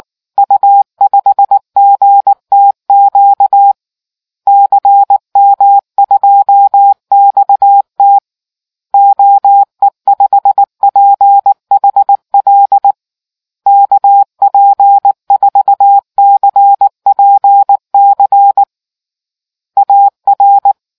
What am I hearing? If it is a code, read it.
Morse code, heard as a keyed tone, about 19 wpm, 800 Hz. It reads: U5GTQ CM2XT OE5PHL KP4CPC AR